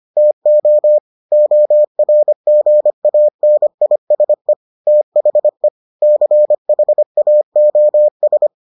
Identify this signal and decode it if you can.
Morse code, heard as a keyed tone, about 25 wpm, 600 Hz. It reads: TO ORGANISE THE CHAOS